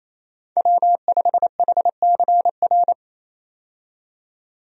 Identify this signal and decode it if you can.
Morse code, heard as a keyed tone, about 28 wpm, 700 Hz. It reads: W5HCR
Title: USA Call Signs - 5 Characters - 28wpm